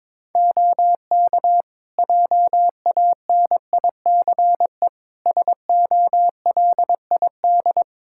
Morse code, heard as a keyed tone, about 21 wpm, 700 Hz. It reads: OK JANICE SOLID
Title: Practice QSO - Exchange 3 - More Info - 22wpm